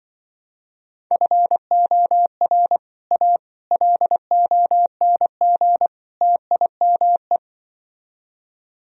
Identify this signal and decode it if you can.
Morse code, heard as a keyed tone, about 24 wpm, 700 Hz. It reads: FOR A LONG TIME
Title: Speed-Racing - Sets of 4 Words 20wpm